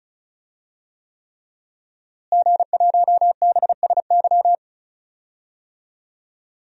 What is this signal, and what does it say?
Morse code, heard as a keyed tone, about 36 wpm, 700 Hz. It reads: G1BSY
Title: DXCC-100 Call Signs - 5 Characters - Repeated 3x - 35wpm